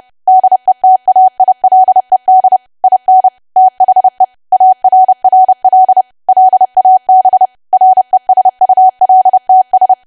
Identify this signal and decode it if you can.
Morse code, heard as a keyed tone, about 30 wpm, 750 Hz. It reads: DETAILED IN THE ARRL LAB RESULTS
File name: cwa / W1AW-Level3 / 30WPM